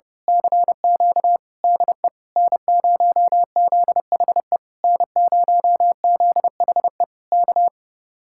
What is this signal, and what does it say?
Morse code, heard as a keyed tone, about 29 wpm, 700 Hz. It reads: CQ DE N0ZHE N0ZHE K
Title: Speed-Racing Common QSO Phrases Encore - 30wpm